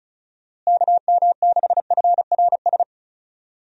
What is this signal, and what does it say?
Morse code, heard as a keyed tone, about 34 wpm, 700 Hz. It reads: KM6FRS
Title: USA Call Signs - 6 Characters - Repeated 3x - 35wpm